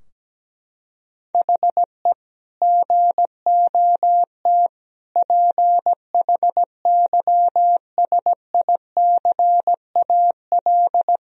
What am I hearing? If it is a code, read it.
Morse code, heard as a keyed tone, about 17 wpm, 700 Hz. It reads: HE GOT PHYSICAL
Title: Sentences from Top 1000 Words - Review Part 4 - 17wpm